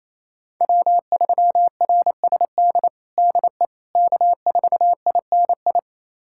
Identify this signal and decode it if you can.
Morse code, heard as a keyed tone, about 28 wpm, 700 Hz. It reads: W3RSD DE K4INI
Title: USA Callsign DE Callsign - 5 Characters - 28wpm